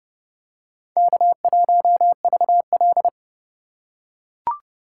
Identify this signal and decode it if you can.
Morse code, heard as a keyed tone, about 29 wpm, 700 Hz. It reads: K1VL